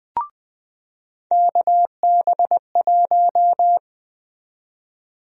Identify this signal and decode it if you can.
Morse code, heard as a keyed tone, about 20 wpm, 700 Hz. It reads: KB1
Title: USA Call Sign Prefixes - 20wpm